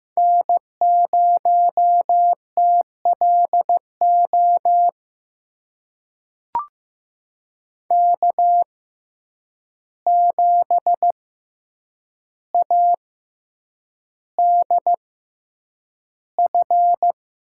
Morse code, heard as a keyed tone, about 15 wpm, spaced as for 8 wpm, 700 Hz. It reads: N0TLO K7ADF
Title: USA Call Signs - 6x Character Spacing - 15wpm